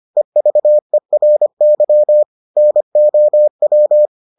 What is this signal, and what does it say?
Morse code, heard as a keyed tone, about 25 wpm, 600 Hz. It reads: EVERY NOW